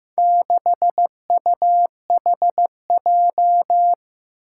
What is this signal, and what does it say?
Morse code, heard as a keyed tone, about 15 wpm, 700 Hz. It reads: 6UHJ
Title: Four Challenging Letter-Number Combinations 15wpm